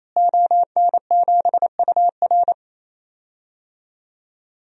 Morse code, heard as a keyed tone, about 28 wpm, 700 Hz. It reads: ON7UR
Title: DXCC-100 Call Signs - 5 Characters - Repeated 3x - 28wpm